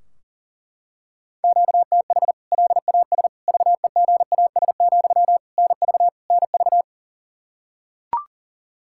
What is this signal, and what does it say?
Morse code, heard as a keyed tone, about 40 wpm, 700 Hz. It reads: QTH LAS VEGAS, NV NV
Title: K1USN SST QSO Practice 40wpm